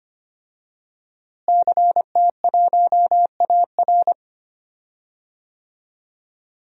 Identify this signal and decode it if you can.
Morse code, heard as a keyed tone, about 25 wpm, 700 Hz. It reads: CT1AR
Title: DXCC-100 Call Signs - 5 Characters - Repeated 3x - 25wpm